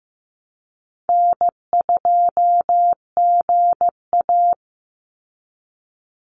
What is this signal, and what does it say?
Morse code, heard as a keyed tone, about 15 wpm, 700 Hz. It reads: N2GA
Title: Easy Call Signs Repeated 3x - 15wpm